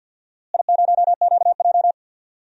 Morse code, heard as a keyed tone, about 51 wpm, 700 Hz. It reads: I0QJ
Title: DXCC-100 Call Signs - 4 Characters - Repeated 2x - 50wpm